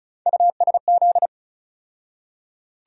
Morse code, heard as a keyed tone, about 34 wpm, 700 Hz. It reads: USZ
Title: USA Call Sign Postfixes - 3 Letters 35wpm